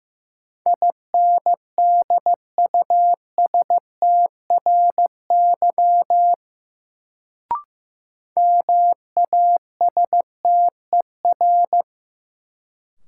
Morse code, heard as a keyed tone, about 15 wpm, 700 Hz. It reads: INDUSTRY MASTER
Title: Sentences from Top 1300 Words - New Word Drill Alt - 15wpm